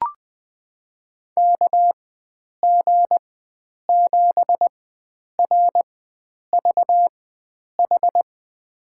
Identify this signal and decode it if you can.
Morse code, heard as a keyed tone, about 20 wpm, spaced as for 9 wpm, 700 Hz. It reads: KG7RVH
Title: USA Call Signs - 6 Characters - 4x Character Spacing - 20wpm